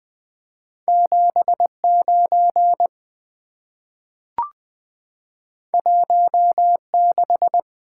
Morse code, heard as a keyed tone, about 20 wpm, 700 Hz. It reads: 79 16
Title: Two Letter-Number Combinations - 20wpm